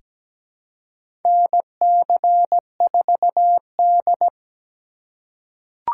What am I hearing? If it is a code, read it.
Morse code, heard as a keyed tone, about 17 wpm, 700 Hz. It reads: NC4D
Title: USA Call Signs - 4 Characters - 3x Character Spacing - 17wpm